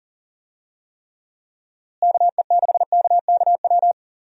Morse code, heard as a keyed tone, about 40 wpm, 700 Hz. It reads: KE6KKW
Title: USA Call Signs - 6 Characters - Repeated 3x - 40wpm